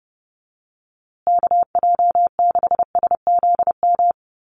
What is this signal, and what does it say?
Morse code, heard as a keyed tone, about 28 wpm, 700 Hz. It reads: KJ6SZM
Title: USA Call Signs Repeated 2x 30wpm